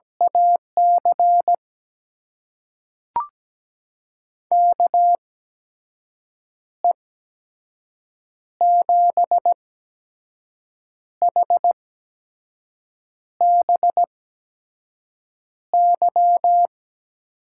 Morse code, heard as a keyed tone, about 17 wpm, spaced as for 5 wpm, 700 Hz. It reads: AC KE7HBY
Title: USA Call Signs - 6 Characters - 8x Character Spacing - 17wpm